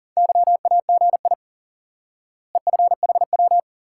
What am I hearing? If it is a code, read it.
Morse code, heard as a keyed tone, about 40 wpm, 700 Hz. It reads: YAGI EFHW
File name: IQE Course 3 - Lesson 89 - YAGI - Two Elements - 40wpm